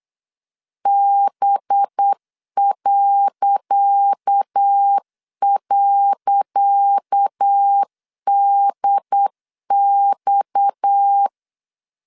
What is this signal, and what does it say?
Morse code, heard as a keyed tone, about 8 wpm, 800 Hz. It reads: B..DX
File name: LEZIONI / 10 SIGNS / 10D COMA 8wpm 3749